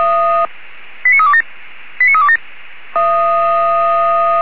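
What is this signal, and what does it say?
Five-tone selective call (ZVEI), heard as quick runs of five tones, tones of about 80 ms. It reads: 89327 89327
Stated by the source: ZVEI five-tone selective call received by SDR.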